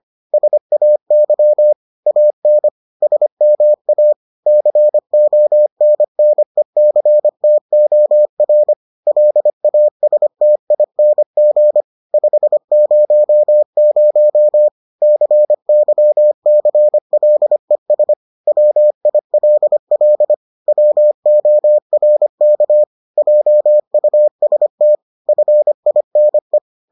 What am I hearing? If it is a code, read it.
Morse code, heard as a keyed tone, about 25 wpm, 600 Hz. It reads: SAY AN SMA CONNECTOR LASTING 500 CYCLES WILL WORK JUST FINE